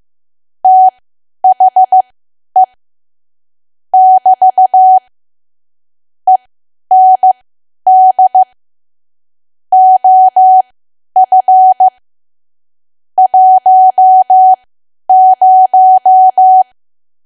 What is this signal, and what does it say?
Morse code, heard as a keyed tone, about 15 wpm, spaced as for 10 wpm, 750 Hz. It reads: THE <BT> END OF 10
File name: cwa / W1AW-Level2 / 10WPM